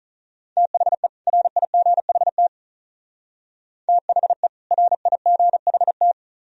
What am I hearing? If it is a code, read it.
Morse code, heard as a keyed tone, about 43 wpm, 700 Hz. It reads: THE RIGHT THE RIGHT
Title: Speed-Racing - Sets of 2 Words 35wpm